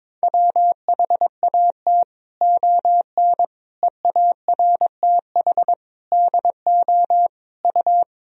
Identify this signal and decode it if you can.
Morse code, heard as a keyed tone, about 21 wpm, 700 Hz. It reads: WHAT ON EARTH DO U
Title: Sentences from Top 200 Words and Abbreviations 22wpm